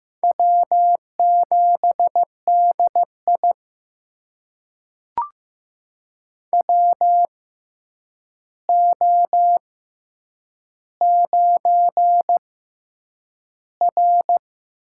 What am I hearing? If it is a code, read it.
Morse code, heard as a keyed tone, about 15 wpm, spaced as for 8 wpm, 700 Hz. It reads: W7DI WO9R
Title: USA Call Signs - 4 Characters - 6x Character Spacing - 15wpm